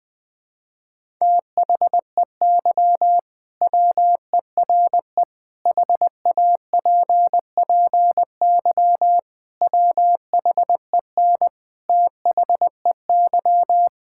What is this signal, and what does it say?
Morse code, heard as a keyed tone, about 20 wpm, 700 Hz. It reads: THEY WERE HAPPY WHEN THEY
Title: Sentences with 2-Word Phrases - 20wpm